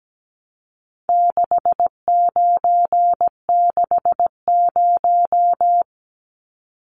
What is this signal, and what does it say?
Morse code, heard as a keyed tone, about 17 wpm, 700 Hz. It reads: 6960
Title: Numbers - 4-Digit 17wpm